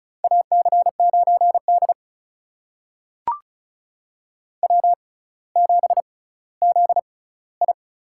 Morse code, heard as a keyed tone, about 36 wpm, spaced as for 18 wpm, 700 Hz. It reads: AC9D W7ZI
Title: USA Call Signs - 4 Characters - 6x Character Spacing - 35wpm